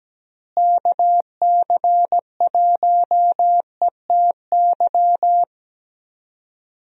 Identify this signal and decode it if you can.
Morse code, heard as a keyed tone, about 17 wpm, 700 Hz. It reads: KC1ETY
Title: USA Call Signs - 6 Characters - Repeated 3x - 17wpm